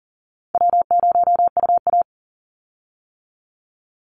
Morse code, heard as a keyed tone, about 41 wpm, 700 Hz. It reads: W0UA
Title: Call Signs - 40wpm